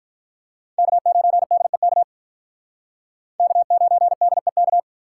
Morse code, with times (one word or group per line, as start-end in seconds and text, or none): 0.8-2.0 K9DEK
3.4-4.8 K9DEK